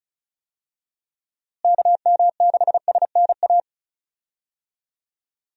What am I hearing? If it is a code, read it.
Morse code, heard as a keyed tone, about 36 wpm, 700 Hz. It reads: KM6SNA
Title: USA Call Signs - 6 Characters - Repeated 3x - 35wpm